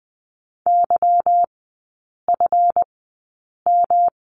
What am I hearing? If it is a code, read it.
Morse code, heard as a keyed tone, about 20 wpm, spaced as for 8 wpm, 690 Hz. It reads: YFM
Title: 145 Lesson 24e - Letter y - Three Characters 20wpm